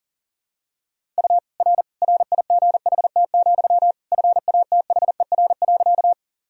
Morse code, heard as a keyed tone, about 40 wpm, 700 Hz. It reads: U R RIGHT, FATHER.
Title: Sentences from Top 300 Words and Abbreviations 40wpm